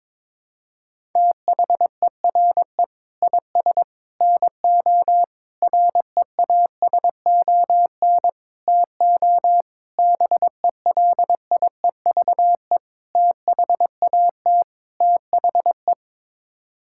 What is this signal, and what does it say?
Morse code, heard as a keyed tone, about 21 wpm, 700 Hz. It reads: THERE IS NO REASON TO BELIEVE THAT THE
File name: practice / 22wpm / Sets of 8 Words - 22wpm